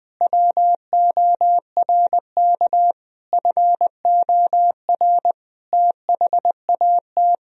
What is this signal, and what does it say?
Morse code, heard as a keyed tone, about 20 wpm, 700 Hz. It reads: WORK FOR THAT
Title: Sentences from Top 1000 Words - Review Part 3 - 20wpm